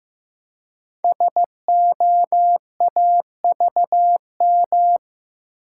Morse code, heard as a keyed tone, about 15 wpm, 700 Hz. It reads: SOAVM
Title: Code Groups - 5 Letters - 15wpm